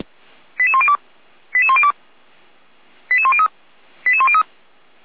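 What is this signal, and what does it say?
Five-tone selective call (ZVEI), heard as quick runs of five tones, tones of about 80 ms. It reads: 88102 88102 88103 88103
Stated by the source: ZVEI five-tone call received on SDR.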